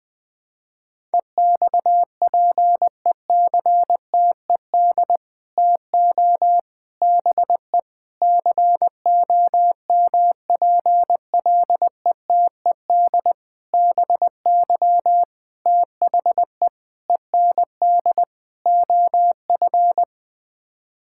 Morse code, heard as a keyed tone, about 20 wpm, 700 Hz. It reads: EXPECTED TO BE COMPLETED BY THE END OF
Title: Sets of 8 Words - 20wpm